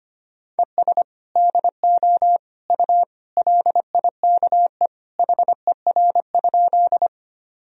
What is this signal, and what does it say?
Morse code, heard as a keyed tone, about 25 wpm, 700 Hz. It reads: ES DO U LIKE HER?
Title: Sentences from Top 100 Words and Abbreviations 25wpm